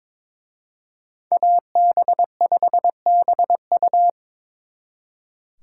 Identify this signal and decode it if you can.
Morse code, heard as a keyed tone, about 26 wpm, 700 Hz. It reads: AB5BU
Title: Speed Racing - USA 2x2 Call Signs 22wpm